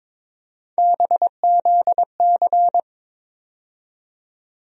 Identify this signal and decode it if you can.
Morse code, heard as a keyed tone, about 22 wpm, 700 Hz. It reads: BZC
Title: USA Call Sign Postfixes - 3 Letters 22wpm